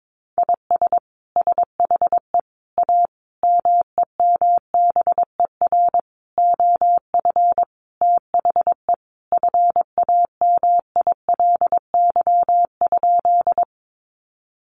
Morse code, heard as a keyed tone, about 22 wpm, 700 Hz. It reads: IS SHE A MEMBER OF THE FAMILY?
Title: Sentences from Top 1100 Words 22wpm